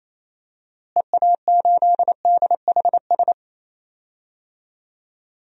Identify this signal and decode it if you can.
Morse code, heard as a keyed tone, about 28 wpm, 700 Hz. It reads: EA8DHS
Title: DXCC-100 Call Signs - 6 Characters - Repeated 2x - 28wpm